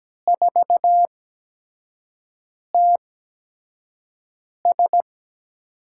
Morse code, heard as a keyed tone, about 17 wpm, spaced as for 5 wpm, 700 Hz. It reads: 4TS
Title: USA Call Signs - 8x Character Spacing - 17wpm